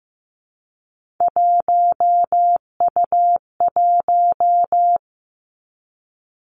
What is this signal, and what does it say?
Morse code, heard as a keyed tone, about 15 wpm, 700 Hz. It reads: 1U1
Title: Three Challenging Letter-Number Combinations 15wpm